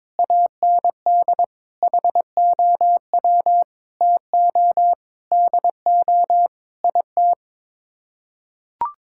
Morse code, heard as a keyed tone, about 22 wpm, 700 Hz. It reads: AND HOW TO DO IT
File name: practice / 22wpm / Sets of 8 Words - 22wpm